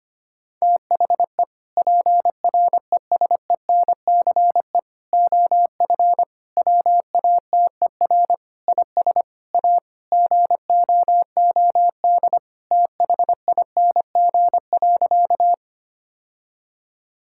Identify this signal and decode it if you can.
Morse code, heard as a keyed tone, about 25 wpm, 700 Hz. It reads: THE PRESENCE OF WATER IS A GOOD THING.
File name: practice / 25wpm / Sentences with 3-Word Phrases - 25wpm